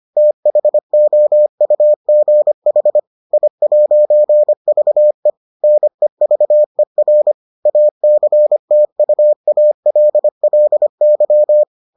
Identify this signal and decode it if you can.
Morse code, heard as a keyed tone, about 25 wpm, 600 Hz. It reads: THOUGH I'VE NEVER ACTUALLY